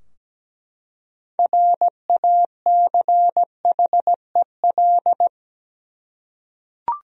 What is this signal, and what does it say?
Morse code, heard as a keyed tone, about 17 wpm, 700 Hz. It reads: RACHEL
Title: USA Common Names - 17wpm